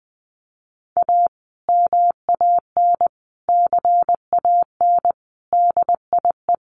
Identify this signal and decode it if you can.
Morse code, heard as a keyed tone, about 20 wpm, 700 Hz. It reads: A MAN CAN DIE
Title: Sentences from Top 1300 Words 20wpm